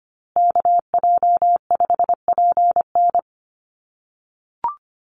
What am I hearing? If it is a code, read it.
Morse code, heard as a keyed tone, about 25 wpm, 700 Hz. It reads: KJ5PN
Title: USA Call Signs Repeated 2x 25wpm